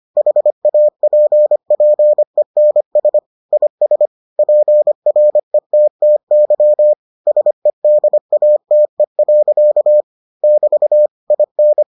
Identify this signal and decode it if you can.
Morse code, heard as a keyed tone, about 25 wpm, 600 Hz. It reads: HAPPENS IS PRETTY SEDATE. <BT> IN